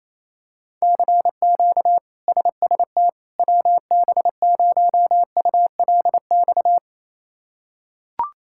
Morse code, heard as a keyed tone, about 27 wpm, 700 Hz. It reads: CQ SST WB0ULX
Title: K1USN SST QSO Practice - 28wpm